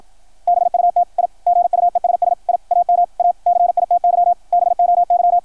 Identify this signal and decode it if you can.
Morse code, heard as a keyed tone, about 53 wpm, 700 Hz, with radio noise. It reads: BUT I GUESS I AM A CITY BOY
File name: archived / CRSnet